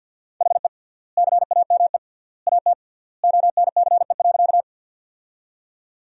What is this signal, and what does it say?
Morse code, heard as a keyed tone, about 51 wpm, 700 Hz. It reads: HE CAME AT ONCE.
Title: 3x Word-Spacing - Sentences from Top 300 Words - 50wpm